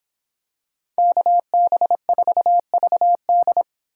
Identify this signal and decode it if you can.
Morse code, heard as a keyed tone, about 25 wpm, 700 Hz. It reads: KB4VD